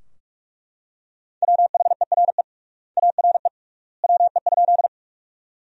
Morse code, heard as a keyed tone, about 42 wpm, 700 Hz. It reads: WHERE ARE WE?
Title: 3x Word-Spacing - Sentences from Top 200 Words - 45wpm